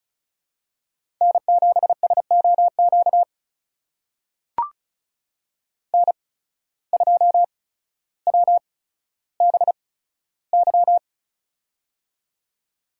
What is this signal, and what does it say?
Morse code, heard as a keyed tone, about 36 wpm, spaced as for 15 wpm, 700 Hz. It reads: N7SOQ N2WBY
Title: USA Call Signs - 5 Characters - 8x Character Spacing - 35wpm